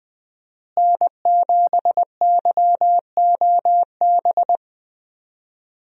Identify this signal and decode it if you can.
Morse code, heard as a keyed tone, about 21 wpm, 700 Hz. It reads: N7YOB
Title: USA Call Signs - 5 Characters - Repeated 2x - 20wpm